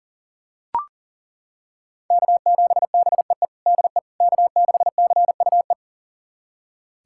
Morse code, heard as a keyed tone, about 37 wpm, 700 Hz. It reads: K7BEE DE K6CUE
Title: USA Callsign DE Callsign - 5 Characters - 40wpm